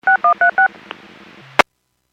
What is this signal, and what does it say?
DTMF, heard as telephone touch tones, tones of about 110 ms. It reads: 6136
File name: WPS / 00593 08-13 WPS 451.25 WPS